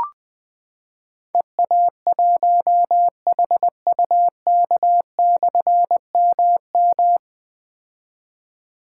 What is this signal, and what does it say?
Morse code, heard as a keyed tone, about 20 wpm, 700 Hz. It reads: EA1HUK/MM
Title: Difficult Call Signs 20wpm